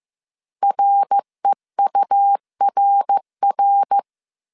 Morse code, heard as a keyed tone, about 15 wpm, 800 Hz. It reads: REURR